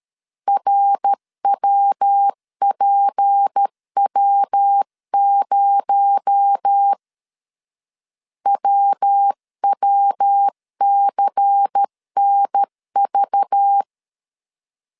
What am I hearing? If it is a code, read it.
Morse code, heard as a keyed tone, about 13 wpm, 800 Hz. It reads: RWPW0 WWCNV
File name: CW / DET / 1 6 1500C S C 12wpm 3533